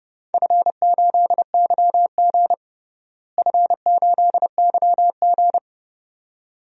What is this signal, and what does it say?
Morse code, heard as a keyed tone, about 30 wpm, 700 Hz. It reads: F8YG F8YG
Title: DXCC-100 Call Signs - 4 Characters - Repeated 2x - 30wpm